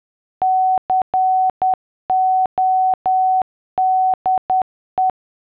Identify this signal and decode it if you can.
Morse code, heard as a keyed tone, about 10 wpm, 750 Hz. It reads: CODE